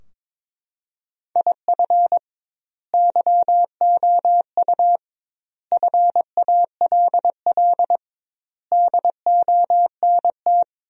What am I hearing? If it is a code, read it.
Morse code, heard as a keyed tone, about 22 wpm, 700 Hz. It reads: IF YOU FALL DONT
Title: 2x Word-Spacing - Sentences from Top 500 Words - 22wpm